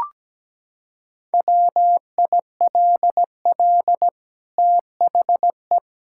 Morse code, heard as a keyed tone, about 17 wpm, 700 Hz. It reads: WILL THE
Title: Sentences from Top 1000 Words - Review Part 5 - 17wpm